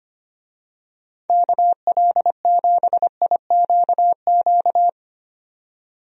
Morse code, heard as a keyed tone, about 25 wpm, 700 Hz. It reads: KL7IQQ
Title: DXCC-100 Call Signs - 6 Characters - 25wpm